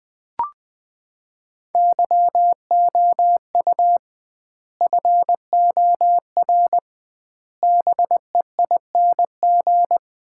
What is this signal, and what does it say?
Morse code, heard as a keyed tone, about 20 wpm, 700 Hz. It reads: YOU FOR BEING
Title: 2x Word-Spacing - Sets of 5 Words - 20wpm